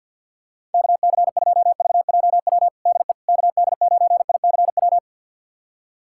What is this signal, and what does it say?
Morse code, heard as a keyed tone, about 50 wpm, 700 Hz. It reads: KC2VJW DE KD9ICW
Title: USA Callsign DE Callsign 50wpm